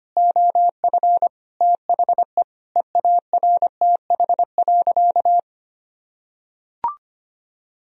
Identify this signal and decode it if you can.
Morse code, heard as a keyed tone, about 25 wpm, 700 Hz. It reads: OF THE EARTH.
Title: Sentences from Top 1000 Words - Review Part 3 - 25wpm